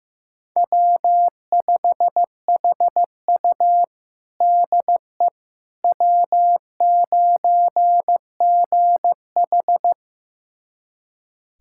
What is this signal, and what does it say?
Morse code, heard as a keyed tone, about 15 wpm, 700 Hz. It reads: W5HU DE W9GH
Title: USA Callsign DE Callsign - 4 Characters - 15wpm